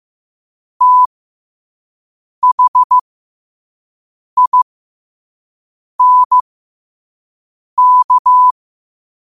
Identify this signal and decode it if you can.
Morse code, heard as a keyed tone, about 15 wpm, spaced as for 5 wpm, 1000 Hz. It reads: THINK
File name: qotdfiles / 5wpm